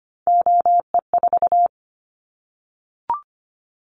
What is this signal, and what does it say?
Morse code, heard as a keyed tone, about 25 wpm, 700 Hz. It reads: OE4
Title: Callsign Prefixes for Top 25 DXCC - 25wpm